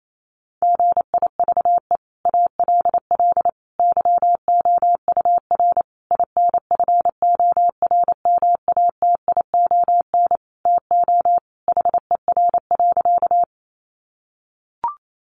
Morse code, heard as a keyed tone, about 28 wpm, 700 Hz. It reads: GIVE ALL YOUR INFORMATION TO HER.
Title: Sentences from Top 600 Words 28wpm